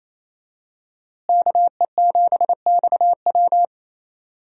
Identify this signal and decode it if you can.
Morse code, heard as a keyed tone, about 28 wpm, 700 Hz. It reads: KE7XW
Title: USA Call Signs - 5 Characters - Repeated 2x - 28wpm